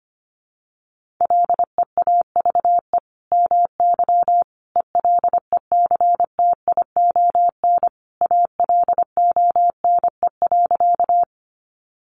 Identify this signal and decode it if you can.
Morse code, heard as a keyed tone, about 25 wpm, 700 Hz. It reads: LEAVE MY ELECTION ALONE.
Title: Sentences from Top 1500 Words 25wpm